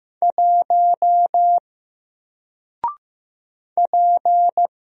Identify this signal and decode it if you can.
Morse code, heard as a keyed tone, about 15 wpm, 700 Hz. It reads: 1 P
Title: Single Letter-Number 15wpm